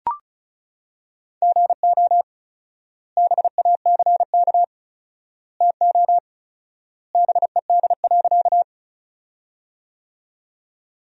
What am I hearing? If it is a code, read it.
Morse code, heard as a keyed tone, about 34 wpm, 700 Hz. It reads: GO BACK TO BED.
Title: 4x Word-Spacing - Sentences from Top 500 Words - 35wpm